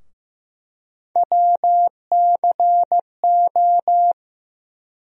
Morse code, heard as a keyed tone, about 15 wpm, 700 Hz. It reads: WCO
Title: USA Call Sign Postfixes - 3 Letters 15wpm